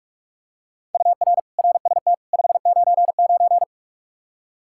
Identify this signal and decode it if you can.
Morse code, heard as a keyed tone, about 46 wpm, 700 Hz. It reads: UR RST 599
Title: Practice QSO - Exchange 1 - Beginning - 45wpm